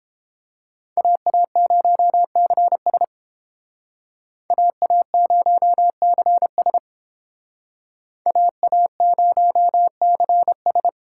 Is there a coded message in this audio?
Yes: Morse code from 1.0 to 10.9 s, heard as a keyed tone, about 31 wpm, 700 Hz. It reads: AA0CS AA0CS AA0CS